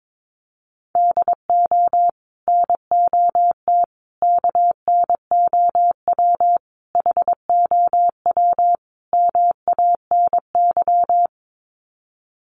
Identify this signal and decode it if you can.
Morse code, heard as a keyed tone, about 22 wpm, 700 Hz. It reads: DO NOT KNOW HOW MANY
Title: Sets of 5 Words Encore 22wpm